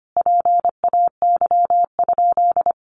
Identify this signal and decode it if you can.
Morse code, heard as a keyed tone, about 25 wpm, 700 Hz. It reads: PAY?
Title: Sentences from Top 700 Words 25wpm